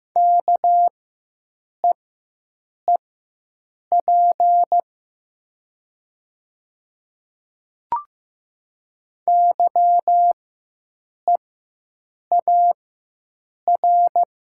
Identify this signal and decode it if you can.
Morse code, heard as a keyed tone, about 15 wpm, spaced as for 7 wpm, 700 Hz. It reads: KEEP YEAR
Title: 4x Character Spacing - Top 300 Words - 15wpm